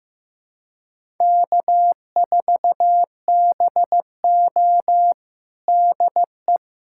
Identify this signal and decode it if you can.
Morse code, heard as a keyed tone, about 15 wpm, 700 Hz. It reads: K4BO DE
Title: USA Callsign DE Callsign - 4 Characters - 15wpm